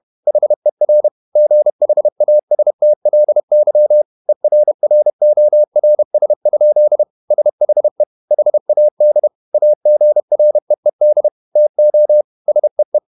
Morse code, heard as a keyed tone, about 31 wpm, 600 Hz. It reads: HER GHASTLY ERRORS? SHE HAD AGREED TO SEE
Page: Never 31wpm | CW med Gnister